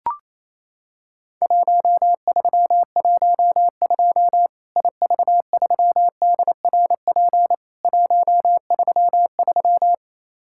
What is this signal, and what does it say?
Morse code, heard as a keyed tone, about 28 wpm, 700 Hz. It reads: 1312 IV3DRP 133
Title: Worked All Europe DX Contest - QTC Exchange - 28wpm